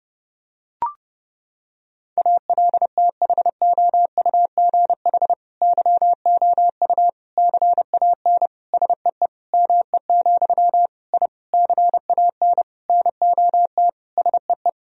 Morse code, heard as a keyed tone, about 30 wpm, 700 Hz. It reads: ALTHOUGH YOU CAN SEE ME, I CAN NOT SEE
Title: Sentences from Top 1000 Words - Review Part 3 - 30wpm